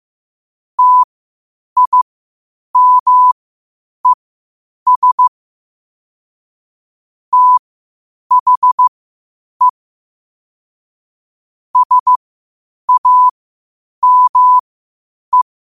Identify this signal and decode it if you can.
Morse code, heard as a keyed tone, about 15 wpm, spaced as for 8 wpm, 1000 Hz. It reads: TIMES THE SAME
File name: qotdfiles / 8wpm